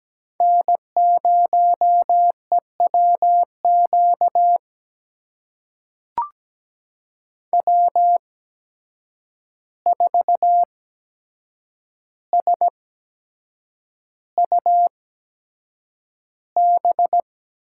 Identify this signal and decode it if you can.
Morse code, heard as a keyed tone, about 17 wpm, spaced as for 7 wpm, 700 Hz. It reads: N0EWQ W4SUB
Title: USA Call Signs - 5 Characters - 8x Character Spacing - 17wpm